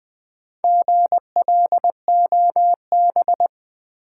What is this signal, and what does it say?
Morse code, heard as a keyed tone, about 20 wpm, 700 Hz. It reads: GLOB
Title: Build A Word - 2 Letters At A Time - 6-Letter Words - 20wpm